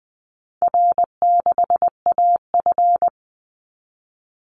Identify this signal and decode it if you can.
Morse code, heard as a keyed tone, about 20 wpm, 700 Hz. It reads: R6AF
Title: Easy Call Signs Repeated 3x - 20wpm